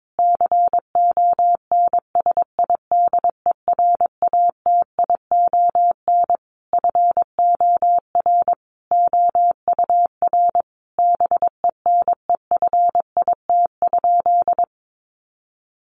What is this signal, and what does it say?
Morse code, heard as a keyed tone, about 22 wpm, 700 Hz. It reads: CONSIDERATION FOR OUR BENEFIT?
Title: Sentences from Top 1500 Words 22wpm